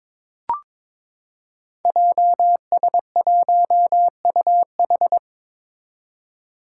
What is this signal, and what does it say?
Morse code, heard as a keyed tone, about 22 wpm, 700 Hz. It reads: JS1UH
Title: Five Challenging Letter-Number Combinations 22wpm